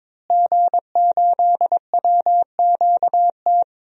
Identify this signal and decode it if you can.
Morse code, heard as a keyed tone, about 22 wpm, 700 Hz. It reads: G8WQT
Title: DXCC-100 Call Signs - 5 Characters - Repeated 2x - 22wpm